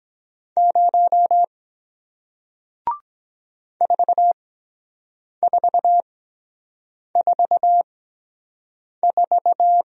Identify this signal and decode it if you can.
Morse code, heard as a keyed tone, about 23 wpm, spaced as for 7 wpm, 700 Hz. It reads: 0 4444